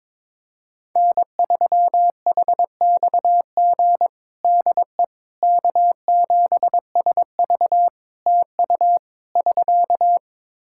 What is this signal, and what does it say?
Morse code, heard as a keyed tone, about 22 wpm, 700 Hz. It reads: N3HXG DE K7SV TU <SK>
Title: K1USN SST QSO Practice 22wpm